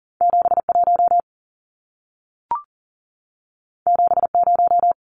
Morse code, heard as a keyed tone, about 40 wpm, 700 Hz. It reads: 71 70
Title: Numbers - 2-Digit 40wpm